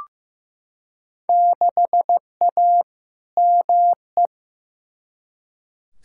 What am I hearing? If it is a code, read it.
Morse code, heard as a keyed tone, about 15 wpm, 700 Hz. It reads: 6A ME
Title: Field Day Exchanges - 15wpm